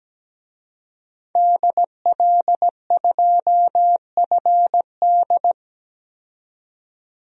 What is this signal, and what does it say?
Morse code, heard as a keyed tone, about 17 wpm, 700 Hz. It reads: DL2FD
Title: DXCC-100 Call Signs - 5 Characters - 17wpm